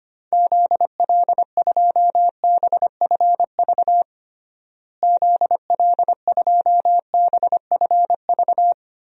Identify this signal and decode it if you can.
Morse code, heard as a keyed tone, about 25 wpm, 700 Hz. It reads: ZL2BFV ZL2BFV
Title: DXCC-100 Call Signs - 6 Characters - Repeated 3x - 25wpm